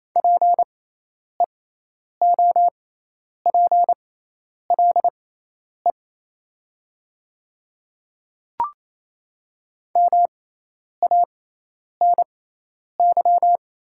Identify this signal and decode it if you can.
Morse code, heard as a keyed tone, about 28 wpm, spaced as for 10 wpm, 700 Hz. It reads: PEOPLE MANY
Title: 6x Character Spacing - Top 100 Words - 28wpm